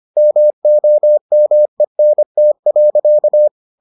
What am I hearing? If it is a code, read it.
Morse code, heard as a keyed tone, about 25 wpm, 600 Hz. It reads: MOMENT.